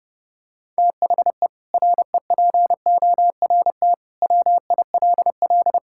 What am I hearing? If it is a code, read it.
Morse code, heard as a keyed tone, about 32 wpm, 700 Hz. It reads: THE REPORT WILL
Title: Sentences from Top 1000 Words - Review Part 6 - 30wpm